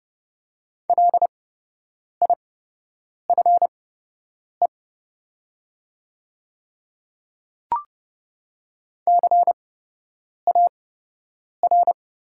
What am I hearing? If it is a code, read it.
Morse code, heard as a keyed tone, about 30 wpm, spaced as for 8 wpm, 700 Hz. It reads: LIFE CAR
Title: 8x Character Spacing - Top 300 Words - 30wpm